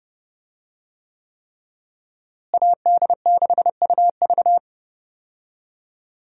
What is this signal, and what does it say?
Morse code, heard as a keyed tone, about 32 wpm, 700 Hz. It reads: AD6UV
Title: USA Call Signs - 5 Characters - Repeated 3x - 30wpm